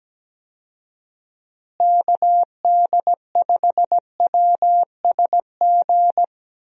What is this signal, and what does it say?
Morse code, heard as a keyed tone, about 17 wpm, 700 Hz. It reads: KD5WSG
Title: USA Call Signs - 6 Characters - Repeated 3x - 17wpm